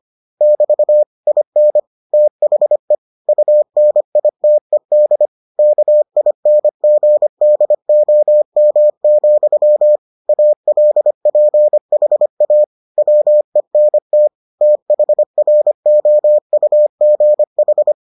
Morse code, heard as a keyed tone, about 25 wpm, 600 Hz. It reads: <BT> IN THE UNITED KINGDOM, ALPHA WENT THROUGH